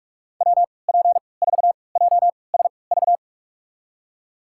Morse code, heard as a keyed tone, about 42 wpm, spaced as for 25 wpm, 700 Hz. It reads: WP4JSV
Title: USA Call Signs - 6 Characters - 3x Character Spacing - 45wpm